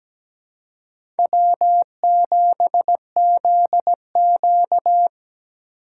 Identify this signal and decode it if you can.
Morse code, heard as a keyed tone, about 17 wpm, 700 Hz. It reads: W7ZQ
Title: USA Call Signs - 8x Character Spacing - 17wpm